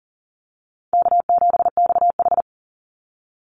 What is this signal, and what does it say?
Morse code, heard as a keyed tone, about 40 wpm, 700 Hz. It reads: K7XH
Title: Call Signs Repeated 3x - 40wpm